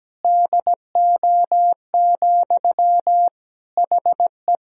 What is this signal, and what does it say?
Morse code, heard as a keyed tone, about 17 wpm, 700 Hz. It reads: DO, HE
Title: Sentences from Top 100 Words and Abbreviations 17wpm